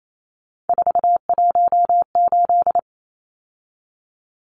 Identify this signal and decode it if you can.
Morse code, heard as a keyed tone, about 28 wpm, 700 Hz. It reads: 418
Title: Numbers - 3-Digit 28wpm